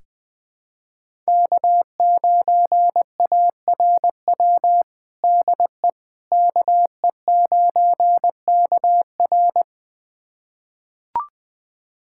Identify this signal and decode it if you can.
Morse code, heard as a keyed tone, about 20 wpm, 700 Hz. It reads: K9ARW DE KE9KR
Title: USA Callsign DE Callsign - 5 Characters - 20wpm